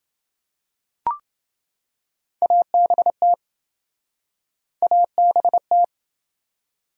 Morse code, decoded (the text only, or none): ABT ABT